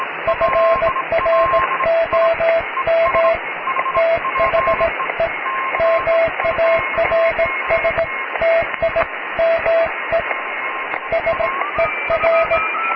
Morse code, heard as a keyed tone, about 17 wpm, 660 Hz, with radio noise. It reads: FROM THE MARS TIME SER
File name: cw / tekst(b)+QRM